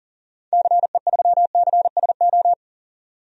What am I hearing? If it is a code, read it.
Morse code, heard as a keyed tone, about 40 wpm, 700 Hz. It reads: CE3CSO
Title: DXCC-100 Call Signs - 6 Characters - 40wpm